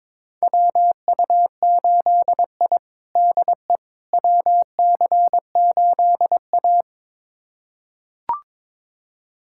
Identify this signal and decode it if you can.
Morse code, heard as a keyed tone, about 22 wpm, 700 Hz. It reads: WU8I DE WC8A
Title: USA Callsign DE Callsign - 4 Characters - 22wpm